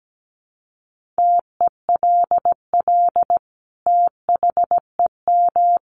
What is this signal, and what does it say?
Morse code, heard as a keyed tone, about 17 wpm, 700 Hz. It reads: TELL THEM
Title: Sentences from Top 1200 Words 17wpm